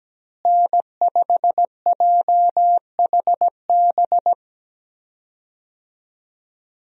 Morse code, heard as a keyed tone, about 17 wpm, 700 Hz. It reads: N5JHB
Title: USA Call Signs - 5 Characters - Repeated 3x - 17wpm